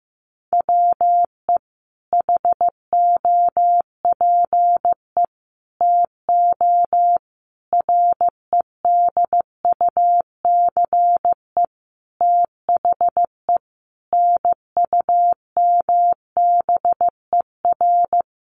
Morse code, heard as a keyed tone, about 15 wpm, 700 Hz. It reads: WE HOPE TO REDUCE THE NUMBER
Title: Sentences from Top 1100 Words 15wpm